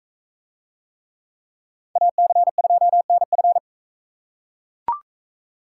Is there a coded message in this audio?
Yes: Morse code from 2.0 to 3.6 s, heard as a keyed tone, about 40 wpm, 700 Hz. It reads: AC2NF